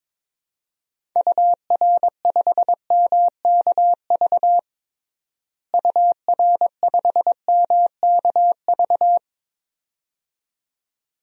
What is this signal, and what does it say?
Morse code, heard as a keyed tone, about 22 wpm, 700 Hz. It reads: UR5MKV UR5MKV
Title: DXCC-100 Call Signs - 6 Characters - Repeated 2x - 22wpm